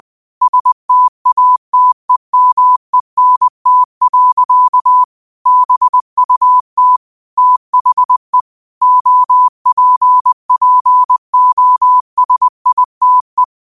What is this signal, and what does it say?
Morse code, heard as a keyed tone, about 20 wpm, 1000 Hz. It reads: STATEMENT. BUT THE OPPOSITE
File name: qotdfiles / 20wpm